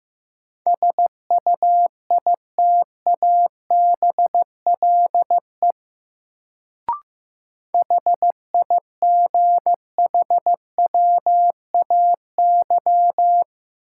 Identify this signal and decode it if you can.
Morse code, heard as a keyed tone, about 15 wpm, 700 Hz. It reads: SUITABLE HIGHWAY
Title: Sentences from Top 1400 Words - New Word Drill - 15wpm